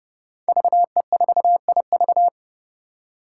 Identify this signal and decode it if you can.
Morse code, heard as a keyed tone, about 30 wpm, 700 Hz. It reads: VE4IV
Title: Five Challenging Letter-Number Combinations 30wpm